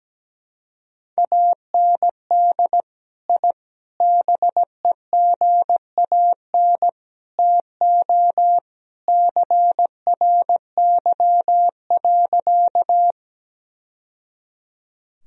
Morse code, heard as a keyed tone, about 17 wpm, 700 Hz. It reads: AND I BEGAN TO CRY.
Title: Sentences from Top 1000 Words - Review Part 2 - 17wpm